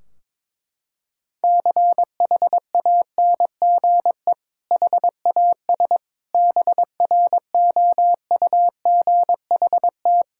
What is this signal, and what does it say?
Morse code, heard as a keyed tone, about 22 wpm, 700 Hz. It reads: CHANGE HAS BROUGHT
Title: Sentences from Top 1600 Words 22wpm